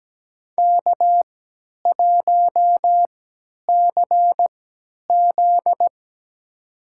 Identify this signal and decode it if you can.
Morse code, heard as a keyed tone, about 17 wpm, spaced as for 10 wpm, 700 Hz. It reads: K1CZ
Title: USA Call Signs - 4 Characters - 3x Character Spacing - 17wpm